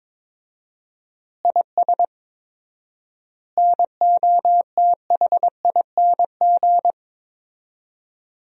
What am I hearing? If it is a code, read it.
Morse code, heard as a keyed tone, about 23 wpm, 700 Hz. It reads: IS NOTHING
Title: 4x Word-Spacing - Sentences from Top 500 Words - 22wpm